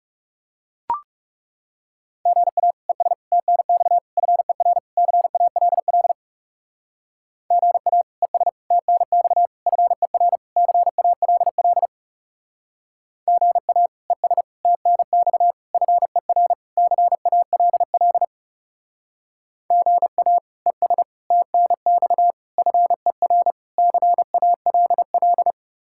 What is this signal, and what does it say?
Morse code, heard as a keyed tone, about 36 wpm, 700 Hz. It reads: GA ES TNX FER CALL GA ES TNX FER CALL GA ES TNX FER CALL GA ES TNX FER CALL
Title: Speed-Racing Common QSO Phrases Encore - 30wpm